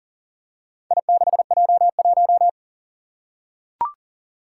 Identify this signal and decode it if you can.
Morse code, heard as a keyed tone, about 40 wpm, 700 Hz. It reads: I6J1
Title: Four Challenging Letter-Number Combinations 40wpm